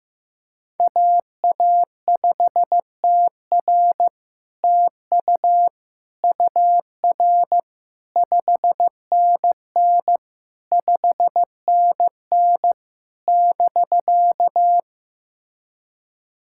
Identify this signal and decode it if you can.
Morse code, heard as a keyed tone, about 15 wpm, 700 Hz. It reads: AA5TR TU UR 5NN 5NN <BK>
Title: SOTA QSO Practice 15wpm